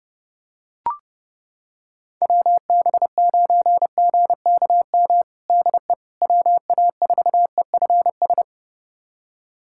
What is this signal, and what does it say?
Morse code, heard as a keyed tone, about 30 wpm, 700 Hz. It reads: WB9GKM DE WA4EFS
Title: USA Callsign DE Callsign - 6 Characters - 30wpm